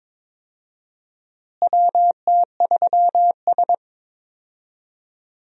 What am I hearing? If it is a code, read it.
Morse code, heard as a keyed tone, about 21 wpm, 700 Hz. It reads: WT3S
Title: USA Call Signs - 4 Characters - Repeated 3x - 22wpm